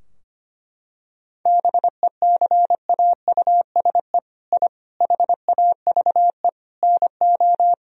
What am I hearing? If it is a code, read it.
Morse code, heard as a keyed tone, about 25 wpm, 700 Hz. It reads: BECAUSE I HAVE NO
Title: Sentences from Top 1000 Words - Review Part 3 - 25wpm